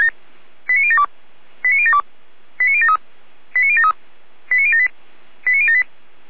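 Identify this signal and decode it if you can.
Five-tone selective call (ZVEI), heard as quick runs of five tones, tones of about 70 ms. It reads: 89072 89072 89073 89073 89078 89078
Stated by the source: ZVEI five-tone selective call 89050 received by SDR.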